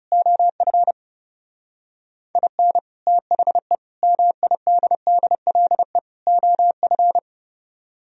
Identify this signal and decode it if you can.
Morse code, heard as a keyed tone, about 38 wpm, 700 Hz. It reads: OF IN THE MIDDLE OF
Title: Speed-Racing - Sets of 4 Words 30wpm